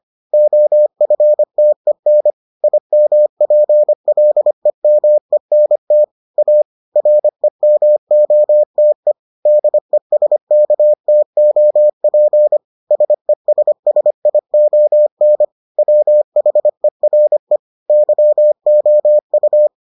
Morse code, heard as a keyed tone, about 25 wpm, 600 Hz. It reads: OFTEN IMPLEMENT A REMOTE DESKTOP SESSION WHERE YOU